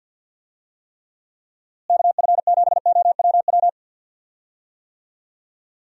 Morse code, heard as a keyed tone, about 50 wpm, 700 Hz. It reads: KF7OWW
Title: USA Call Signs - 6 Characters - Repeated 3x - 50wpm